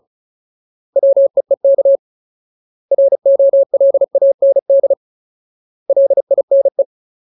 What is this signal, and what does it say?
Morse code, heard as a keyed tone, about 34 wpm, 550 Hz. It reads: WEEK ROLAND LINE
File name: Ultimate Word List / 35WPM Ultimate